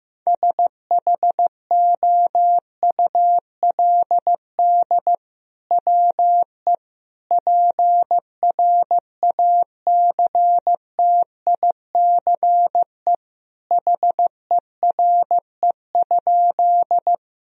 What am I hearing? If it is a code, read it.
Morse code, heard as a keyed tone, about 16 wpm, 700 Hz. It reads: SHOULD WE PRACTICE HERE?
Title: Sentences from Top 1000 Words - Review Part 4 - 15wpm